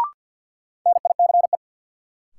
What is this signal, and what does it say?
Morse code, heard as a keyed tone, about 52 wpm, 700 Hz. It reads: NICE
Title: Sentences from Top 1100 Words - New Word Drill Alt - 50wpm